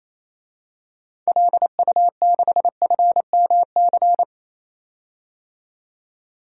Morse code, heard as a keyed tone, about 28 wpm, 700 Hz. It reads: LU6FMC
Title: DXCC-100 Call Signs - 6 Characters - 28wpm